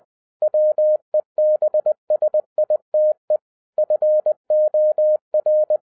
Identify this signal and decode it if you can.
Morse code, heard as a keyed tone, about 20 wpm, 610 Hz. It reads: WEBSITE FOR
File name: 2011-07-mp3 / n43 Czech Army Morse Keys 20wpm